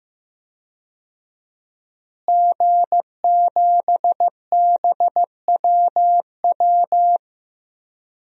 Morse code, heard as a keyed tone, about 15 wpm, 700 Hz. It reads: G7BWW
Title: DXCC-100 Call Signs - 5 Characters - Repeated 2x - 15wpm